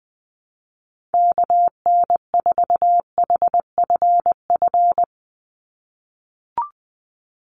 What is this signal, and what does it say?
Morse code, heard as a keyed tone, about 20 wpm, 700 Hz. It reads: KN4HFF
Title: USA Call Signs 20wpm